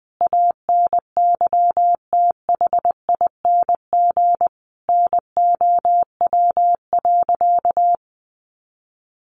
Morse code, heard as a keyed tone, about 20 wpm, 700 Hz. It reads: ANYTHING NOW.
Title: Sentences from Top 1400 Words 20wpm